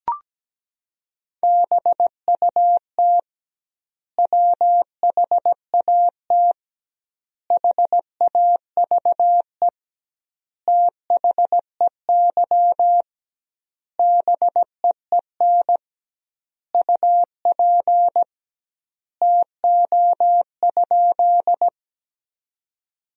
Morse code, heard as a keyed tone, about 17 wpm, 700 Hz. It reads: BUT WHAT HAVE THEY BEEN UP TO?
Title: 2x Word-Spacing - Sentences from Top 100 Words - 17wpm